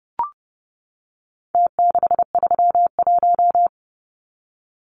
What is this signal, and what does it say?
Morse code, heard as a keyed tone, about 30 wpm, 700 Hz. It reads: T631
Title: Numbers - Contest Style - 4-Digit 30wpm